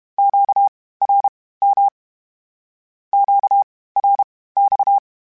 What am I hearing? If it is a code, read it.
Morse code, heard as a keyed tone, about 32 wpm, spaced as for 18 wpm, 800 Hz. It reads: QRM QRX
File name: (spaced) / Q-Calls Spaced-mp3s / Q Calls Twice sp @30